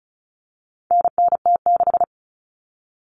Morse code, heard as a keyed tone, about 34 wpm, 700 Hz. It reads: NNT6
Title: Common Cut Numbers - 4-Digit 35wpm